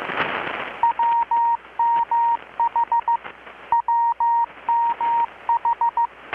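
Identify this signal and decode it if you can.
Morse code, heard as a keyed tone, about 14 wpm, 940 Hz. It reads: WMH WMH
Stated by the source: Morse from KPH's Night of Nights XX maritime radiotelegraphy event, received in Chiapas, Mexico about 3,800 km away on a Kenwood R-600 with a grounded 10 m long-wire antenna.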